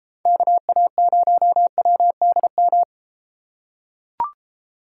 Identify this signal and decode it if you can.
Morse code, heard as a keyed tone, about 32 wpm, 700 Hz. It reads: KA0WDM